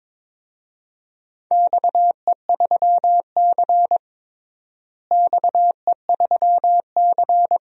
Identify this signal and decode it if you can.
Morse code, heard as a keyed tone, about 21 wpm, 700 Hz. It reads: XE3C XE3C
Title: DXCC-100 Call Signs - 4 Characters - Repeated 2x - 22wpm